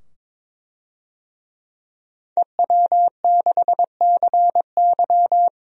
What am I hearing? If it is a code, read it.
Morse code, heard as a keyed tone, about 22 wpm, 700 Hz. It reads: EW6CY
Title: DXCC-100 Call Signs - 5 Characters - 22wpm